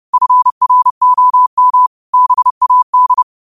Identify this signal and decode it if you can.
Morse code, heard as a keyed tone, about 30 wpm, 1000 Hz. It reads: FROM BAD
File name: qotdfiles / 30wpm